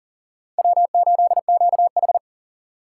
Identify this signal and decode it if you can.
Morse code, heard as a keyed tone, about 37 wpm, 700 Hz. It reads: W8QH
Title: USA Call Signs - 4 Characters - 2x Character Spacing - 40wpm